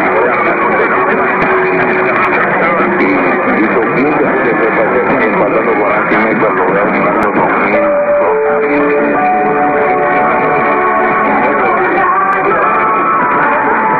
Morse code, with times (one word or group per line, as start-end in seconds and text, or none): none